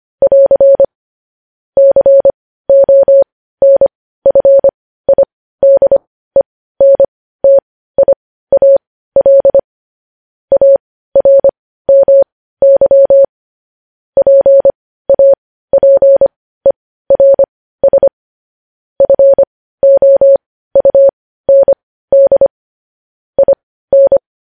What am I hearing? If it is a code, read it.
Morse code, heard as a keyed tone, about 25 wpm, spaced as for 15 wpm, 550 Hz. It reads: <AR> CONFIDENTIAL ARMY PAPERS FOUND IN
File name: podcast / News headlines-1x-25x15